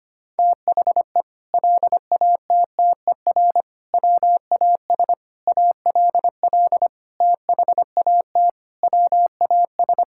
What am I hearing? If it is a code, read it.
Morse code, heard as a keyed tone, about 25 wpm, 700 Hz. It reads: THE LATTER WAS ALL THAT WAS
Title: Sentences from Top 1600 Words 25wpm